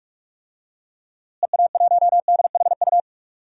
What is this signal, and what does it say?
Morse code, heard as a keyed tone, about 41 wpm, 700 Hz. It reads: EA1DHU
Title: DXCC-100 Call Signs - 6 Characters - Repeated 3x - 45wpm